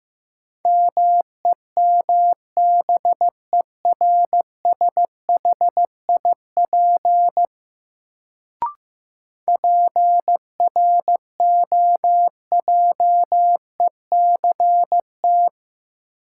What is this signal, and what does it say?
Morse code, heard as a keyed tone, about 15 wpm, 700 Hz. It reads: MEMBERSHIP PROJECT
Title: Sentences from Top 1400 Words - New Word Drill Alt - 15wpm